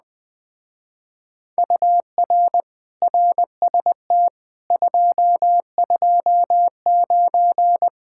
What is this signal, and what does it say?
Morse code, heard as a keyed tone, about 20 wpm, 700 Hz. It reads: UR RST 229
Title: Speed-Racing Common QSO Phrases Encore - 17wpm